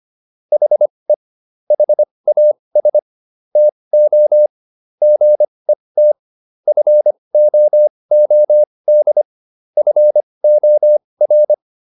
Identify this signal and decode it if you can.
Morse code, heard as a keyed tone, about 25 wpm, spaced as for 20 wpm, 600 Hz. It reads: HE HAS TO GET FOOD FOR